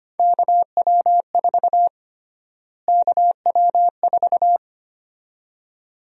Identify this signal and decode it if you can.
Morse code, heard as a keyed tone, about 25 wpm, 700 Hz. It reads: KW4 KW4
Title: USA Call Sign Prefixes - Repeated 2x - 25wpm